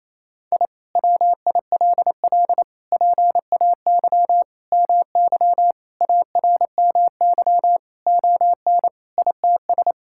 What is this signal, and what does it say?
Morse code, heard as a keyed tone, about 28 wpm, 700 Hz. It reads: I WILL PAY MY ARMY ON ITS
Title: Sentences from Top 1000 Words - Review Part 5 - 28wpm